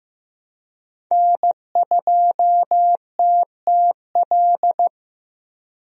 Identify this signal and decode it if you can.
Morse code, heard as a keyed tone, about 15 wpm, 700 Hz. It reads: N2TTL
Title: USA Call Signs - 5 Characters - 15wpm